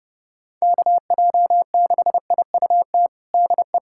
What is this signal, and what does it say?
Morse code, heard as a keyed tone, about 30 wpm, 700 Hz. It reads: KJ6IUT DE
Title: USA Callsign DE Callsign 30wpm